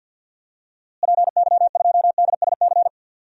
Morse code, heard as a keyed tone, about 51 wpm, 700 Hz. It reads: PY2DSC
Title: DXCC-100 Call Signs - 6 Characters - Repeated 2x - 50wpm